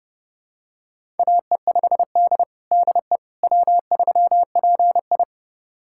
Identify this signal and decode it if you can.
Morse code, heard as a keyed tone, about 30 wpm, 700 Hz. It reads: AE5D DE W3PI
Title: USA Callsign DE Callsign - 4 Characters - 30wpm